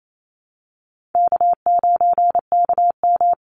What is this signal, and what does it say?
Morse code, heard as a keyed tone, about 28 wpm, 700 Hz. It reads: K9KM
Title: Easy Call Signs Repeated 2x - 28wpm